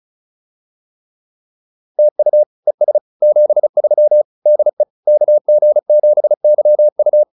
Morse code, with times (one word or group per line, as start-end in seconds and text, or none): 2.0-2.4 TU
2.7-3.0 ES
3.2-4.2 73
4.5-4.8 DE
5.1-7.2 KG7YU